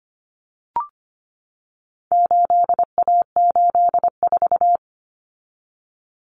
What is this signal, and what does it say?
Morse code, heard as a keyed tone, about 25 wpm, 700 Hz. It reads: 8A84
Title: Common Cut Numbers - 4-Digit 25wpm